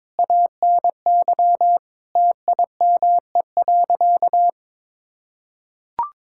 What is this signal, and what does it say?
Morse code, heard as a keyed tone, about 22 wpm, 700 Hz. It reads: ANY TIME.
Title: Sentences from Top 1000 Words - Review Part 6 - 22wpm